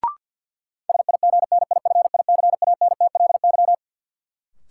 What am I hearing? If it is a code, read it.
Morse code, heard as a keyed tone, about 52 wpm, 700 Hz. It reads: SIGNIFICANTLY
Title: Sentences from Top 1500 Words - New Word Drill Alt - 50wpm